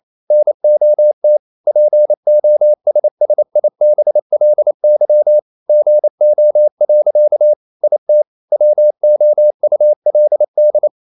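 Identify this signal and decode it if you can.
Morse code, heard as a keyed tone, about 28 wpm, 600 Hz. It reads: NOT POSSIBLY GO. IT WOULD
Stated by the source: Morse code practice at 28 wpm, text of Ken Follett's novel Never in English, from CW med Gnister.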